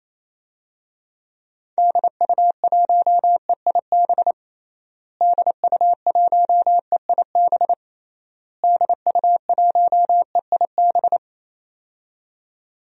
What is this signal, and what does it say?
Morse code, heard as a keyed tone, about 28 wpm, 700 Hz. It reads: DU1EIB DU1EIB DU1EIB
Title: DXCC-100 Call Signs - 6 Characters - Repeated 3x - 28wpm